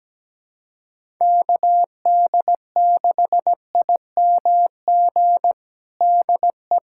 Morse code, heard as a keyed tone, about 17 wpm, 700 Hz. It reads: KD6IMG DE
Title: USA Callsign DE Callsign 17wpm